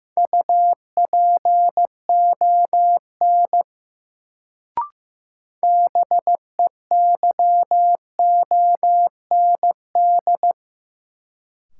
Morse code, heard as a keyed tone, about 15 wpm, 700 Hz. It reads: UPON BEYOND
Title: Sentences from Top 1000 Words - Review Part 6 - New Word Drill - 15wpm